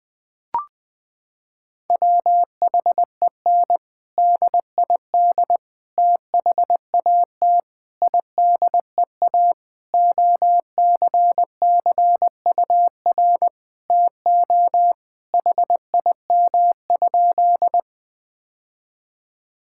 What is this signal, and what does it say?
Morse code, heard as a keyed tone, about 20 wpm, 700 Hz. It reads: WHEN DID THAT IDEA OCCUR TO HIM?
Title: Sentences from Top 1000 Words - Review Part 6 - 20wpm